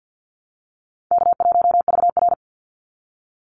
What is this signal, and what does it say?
Morse code, heard as a keyed tone, about 50 wpm, 700 Hz. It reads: K1VR
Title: Easy Call Signs Repeated 3x - 50wpm